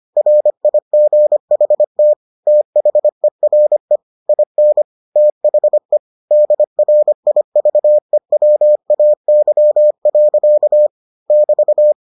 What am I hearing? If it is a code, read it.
Morse code, heard as a keyed tone, about 25 wpm, 600 Hz. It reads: RIGHT THERE IN THE DRIVEWAY. <BT>